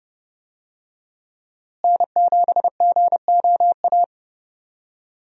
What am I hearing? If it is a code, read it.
Morse code, heard as a keyed tone, about 29 wpm, 700 Hz. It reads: N7GOA
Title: USA Call Signs - 5 Characters - Repeated 2x - 30wpm